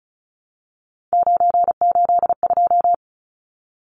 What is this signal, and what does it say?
Morse code, heard as a keyed tone, about 34 wpm, 700 Hz. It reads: 982
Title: Numbers - 3-Digit 35wpm